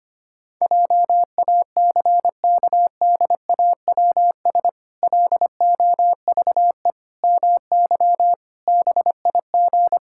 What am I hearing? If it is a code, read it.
Morse code, heard as a keyed tone, about 25 wpm, 700 Hz. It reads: JACKDAWS LOVE MY BIG
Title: Pangram Challenge v2 - 25wpm